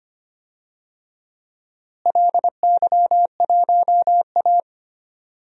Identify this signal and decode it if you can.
Morse code, heard as a keyed tone, about 25 wpm, 700 Hz. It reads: LY1A
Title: DXCC-100 Call Signs - 4 Characters - Repeated 2x - 25wpm